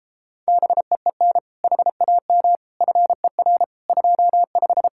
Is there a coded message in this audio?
Yes: Morse code from 0.5 to 4.9 s, heard as a keyed tone, about 33 wpm, 700 Hz. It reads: BEEN HAM FER 25